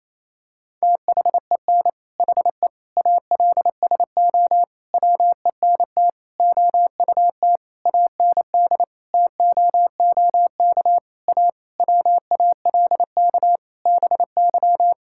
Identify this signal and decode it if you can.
Morse code, heard as a keyed tone, about 28 wpm, 700 Hz. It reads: THEN HE ALSO WENT OUT AND TOOK A WALK BY
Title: Sentences from Top 1600 Words 28wpm